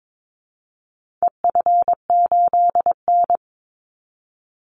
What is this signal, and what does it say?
Morse code, heard as a keyed tone, about 22 wpm, 700 Hz. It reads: EF8N
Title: Call Signs - 22wpm